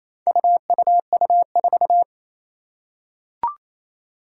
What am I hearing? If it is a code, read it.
Morse code, heard as a keyed tone, about 27 wpm, 700 Hz. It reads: UUU4
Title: Four Challenging Letter-Number Combinations 28wpm